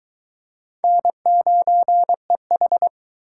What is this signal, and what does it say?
Morse code, heard as a keyed tone, about 22 wpm, 700 Hz. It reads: N9EH